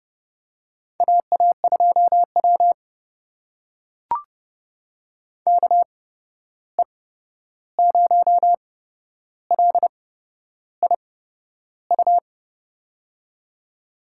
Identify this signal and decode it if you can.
Morse code, heard as a keyed tone, about 30 wpm, spaced as for 11 wpm, 700 Hz. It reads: AA2W KE0LIU
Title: USA Call Signs - 8x Character Spacing - 30wpm